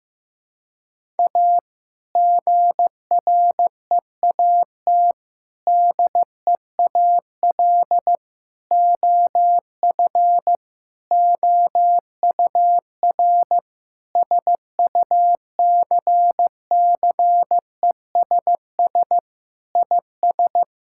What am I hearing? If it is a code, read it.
Morse code, heard as a keyed tone, about 15 wpm, 700 Hz. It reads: A GREAT DEAL OF OUR SUCCESS IS
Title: Sentences from Top 1000 Words - Review Part 6 - 15wpm